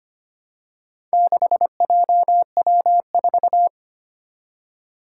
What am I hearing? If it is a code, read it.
Morse code, heard as a keyed tone, about 25 wpm, 700 Hz. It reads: 6JW4
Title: Four Challenging Letter-Number Combinations 25wpm